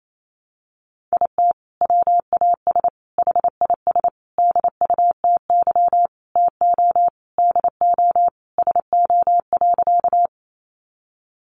Morse code, heard as a keyed tone, about 28 wpm, 700 Hz. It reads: IT WAS HIS DUTY TO DO SO.
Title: Sentences from Top 1500 Words 28wpm